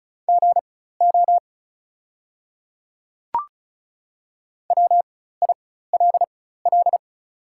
Morse code, heard as a keyed tone, about 36 wpm, spaced as for 16 wpm, 690 Hz. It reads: GO WILL
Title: 4x Character Spacing - Top 100 Words - 35wpm